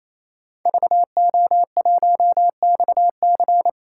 Morse code, heard as a keyed tone, about 28 wpm, 700 Hz. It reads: VO1XC
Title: K1USN SST QSO Practice - 28wpm